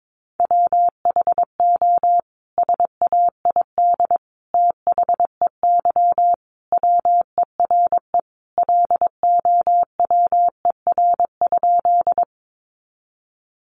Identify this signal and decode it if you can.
Morse code, heard as a keyed tone, about 22 wpm, 700 Hz. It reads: WHO SAID THEY WERE LOWER?
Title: Sentences from Top 700 Words 22wpm